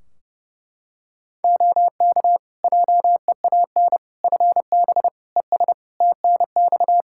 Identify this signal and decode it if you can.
Morse code, heard as a keyed tone, about 32 wpm, 700 Hz. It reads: OK JEAN FB ES TNX
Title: Practice QSO - Exchange 2 - Info - 30wpm